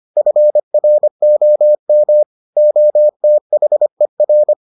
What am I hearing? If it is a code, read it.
Morse code, heard as a keyed tone, about 25 wpm, 600 Hz. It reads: FROM OTHER